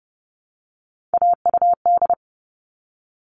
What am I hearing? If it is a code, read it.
Morse code, heard as a keyed tone, about 30 wpm, 700 Hz. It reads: AUD
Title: All Cut Numbers - 3-Digit 30wpm